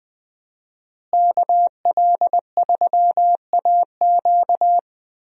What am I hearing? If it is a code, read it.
Morse code, heard as a keyed tone, about 20 wpm, 700 Hz. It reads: KL3AQ
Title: USA Call Signs - 5 Characters - Repeated 3x - 20wpm